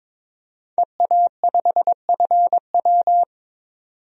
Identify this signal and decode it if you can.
Morse code, heard as a keyed tone, about 22 wpm, 700 Hz. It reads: EA5FW
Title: DXCC-100 Call Signs - 5 Characters - 22wpm